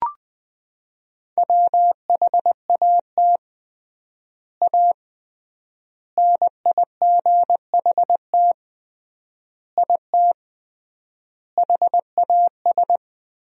Morse code, heard as a keyed tone, about 20 wpm, 700 Hz. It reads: WHAT A NIGHT IT HAS
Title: 3x Word-Spacing - Sentences from Top 300 Words - 20wpm